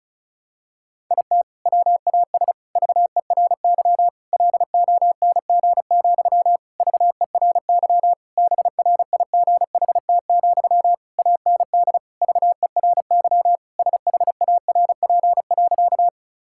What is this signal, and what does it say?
Morse code, heard as a keyed tone, about 34 wpm, 700 Hz. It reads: IT WAS VERY LONG, VERY BRIGHT, AND VERY SHARP.
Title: Sentences from Top 1600 Words 35wpm